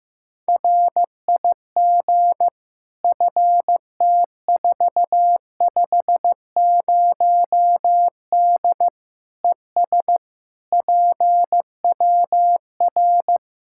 Morse code, heard as a keyed tone, about 15 wpm, 700 Hz. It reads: RIG FT450D ES PWR
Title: Practice QSO - Exchange 2 - Info - 15wpm